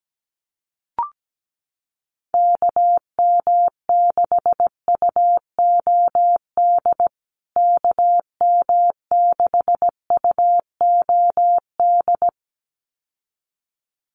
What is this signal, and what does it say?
Morse code, heard as a keyed tone, about 17 wpm, 700 Hz. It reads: KM6UOD KM6UOD
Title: USA Call Signs Repeated 2x 17wpm